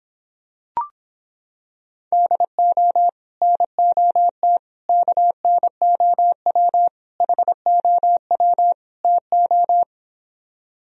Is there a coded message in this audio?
Yes: Morse code from 2.1 to 9.8 s, heard as a keyed tone, about 26 wpm, 700 Hz. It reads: DO NOT KNOW HOW TO